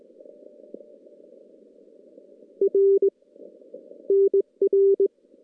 Morse code, heard as a keyed tone, about 20 wpm, 390 Hz. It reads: R NR
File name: contests / CW-SS-k5tr- -2300